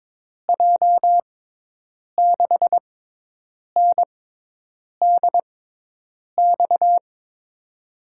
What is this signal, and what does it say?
Morse code, heard as a keyed tone, about 22 wpm, spaced as for 8 wpm, 700 Hz. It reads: J6NDX
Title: USA Call Signs - 6x Character Spacing - 22wpm